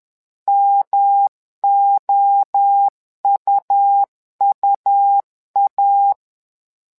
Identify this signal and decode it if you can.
Morse code, heard as a keyed tone, about 11 wpm, 800 Hz. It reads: MOUUA
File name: LEZIONI / 3 MWRO / 3 MWRO 10wpm 2148